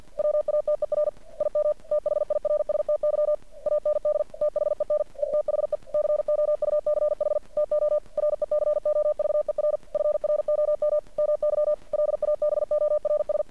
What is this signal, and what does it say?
Morse code, heard as a keyed tone, about 50 wpm, 600 Hz. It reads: WATER IM THIRSTY AND THEN ETHE COUCH TO RECOVER FROM MY LABORS
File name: archived / CRSnet